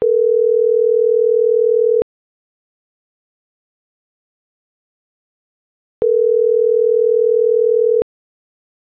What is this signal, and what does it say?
Ringback tone (North America), 440 + 480 Hz.